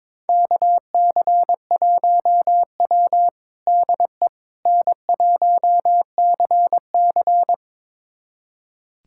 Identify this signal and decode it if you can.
Morse code, heard as a keyed tone, about 22 wpm, 700 Hz. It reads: KC1W DE N1CC
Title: USA Callsign DE Callsign - 4 Characters - 22wpm